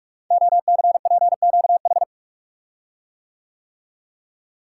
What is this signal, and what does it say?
Morse code, heard as a keyed tone, about 44 wpm, 700 Hz. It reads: OCPQH
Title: Code Groups - 5 Letter:Numbers - 45wpm